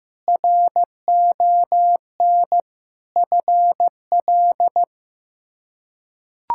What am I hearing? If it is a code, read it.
Morse code, heard as a keyed tone, about 15 wpm, 700 Hz. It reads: RON FL
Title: K1USN SST QSO Practice - 15wpm